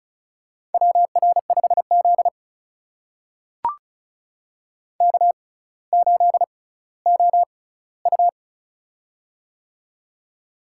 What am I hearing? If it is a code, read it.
Morse code, heard as a keyed tone, about 34 wpm, spaced as for 18 wpm, 700 Hz. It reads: WR5Z K8OU
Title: USA Call Signs - 4 Characters - 6x Character Spacing - 35wpm